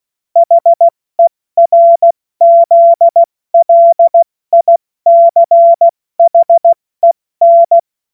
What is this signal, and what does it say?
Morse code, heard as a keyed tone, about 16 wpm, 680 Hz. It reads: HERZLICHEN
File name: Morsekurs / Witze / Witz 28. Dezember 2021, 70 F80